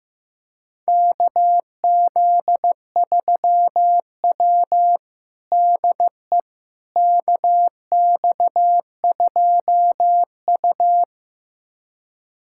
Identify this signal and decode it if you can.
Morse code, heard as a keyed tone, about 16 wpm, 700 Hz. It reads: KZ3W DE KX2U
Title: USA Callsign DE Callsign - 4 Characters - 15wpm